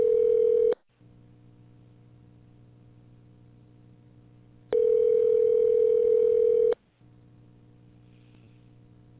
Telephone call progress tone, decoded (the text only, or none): ringback tone